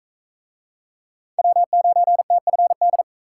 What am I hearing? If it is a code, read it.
Morse code, heard as a keyed tone, about 41 wpm, 700 Hz. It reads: W9TFD